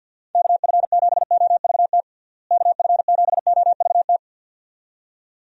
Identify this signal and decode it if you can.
Morse code, heard as a keyed tone, about 53 wpm, 700 Hz. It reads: KF7OVT KF7OVT
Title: USA Call Signs - 6 Characters - Repeated 2x - 50wpm